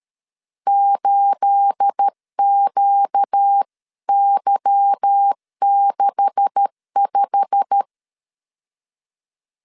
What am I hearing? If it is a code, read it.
Morse code, heard as a keyed tone, about 13 wpm, 800 Hz. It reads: 8QY65
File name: CW / DET / 1 9 1500C S A 12wpm 3800